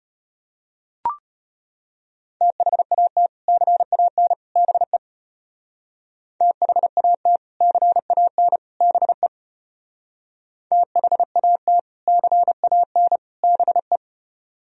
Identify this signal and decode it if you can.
Morse code, heard as a keyed tone, about 31 wpm, 700 Hz. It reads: THAT CAN BE THAT CAN BE THAT CAN BE
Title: Speed-Racing - Sets of 3 Words 25wpm